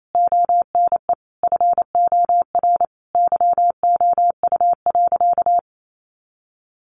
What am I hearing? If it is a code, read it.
Morse code, heard as a keyed tone, about 28 wpm, 700 Hz. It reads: ONE FOR YOU.
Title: Sentences from Top 1500 Words 28wpm